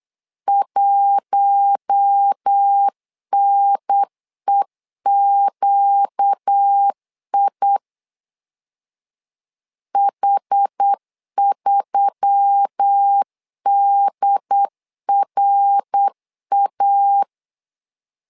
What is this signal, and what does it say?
Morse code, heard as a keyed tone, about 8 wpm, 800 Hz. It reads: 1NEQI H3DRA
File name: LEZIONI / 4 Q1S3 8wpm 3037